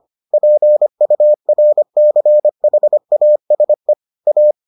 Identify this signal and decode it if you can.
Morse code, heard as a keyed tone, about 25 wpm, 600 Hz. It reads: PURCHASE A